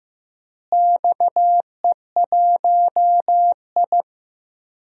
Morse code, heard as a keyed tone, about 15 wpm, 700 Hz. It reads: XE1I
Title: DXCC-100 Call Signs - 4 Characters - Repeated 3x - 15wpm